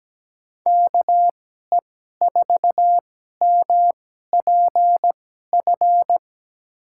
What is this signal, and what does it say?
Morse code, heard as a keyed tone, about 17 wpm, spaced as for 12 wpm, 700 Hz. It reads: KE4MPF
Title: USA Call Signs - 6 Characters - 2x Character Spacing - 17wpm